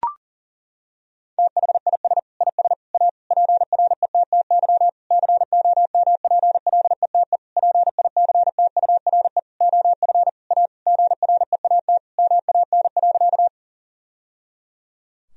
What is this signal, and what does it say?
Morse code, heard as a keyed tone, about 40 wpm, 700 Hz. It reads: THIS IS A PRETTY COMPLETE PICTURE OF A GREAT MAN.
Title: Sentences from Top 1000 Words - Review Part 2 - 40wpm